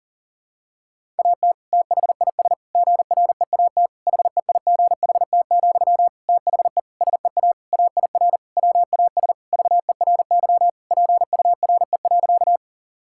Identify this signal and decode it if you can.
Morse code, heard as a keyed tone, about 40 wpm, 700 Hz. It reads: AT THIS GREAT HEIGHT, THE SEA AIR WAS VERY PURE.
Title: Sentences from Top 1600 Words 40wpm